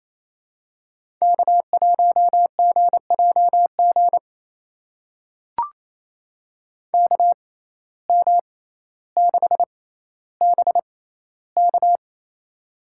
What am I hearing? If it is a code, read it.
Morse code, heard as a keyed tone, about 28 wpm, spaced as for 14 wpm, 700 Hz. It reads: K1GJG KM6BK
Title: USA Call Signs - 5 Characters - 6x Character Spacing - 28wpm